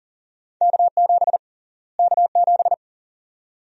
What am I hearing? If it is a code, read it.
Morse code, heard as a keyed tone, about 40 wpm, 700 Hz. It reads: K7 K7
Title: USA Call Sign Prefixes - Repeated 2x - 40wpm